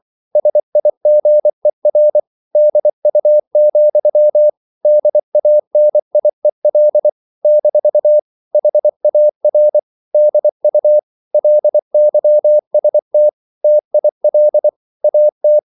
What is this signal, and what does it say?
Morse code, heard as a keyed tone, about 24 wpm, 600 Hz. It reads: SIGER DU, DANIEL - HAR DU LYST TIL AT
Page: Ferie 24 wpm